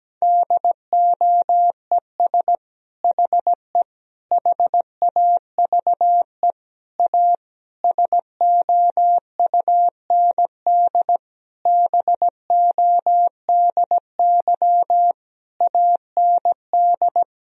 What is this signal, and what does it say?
Morse code, heard as a keyed tone, about 17 wpm, 700 Hz. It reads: DOES HE HAVE A SOUND BODY AND
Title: Sentences from Top 1000 Words - Review Part 2 - 17wpm